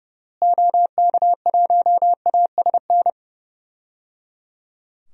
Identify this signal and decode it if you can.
Morse code, heard as a keyed tone, about 29 wpm, 700 Hz. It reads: OK1ASN
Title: DXCC-100 Call Signs - 6 Characters - Repeated 3x - 30wpm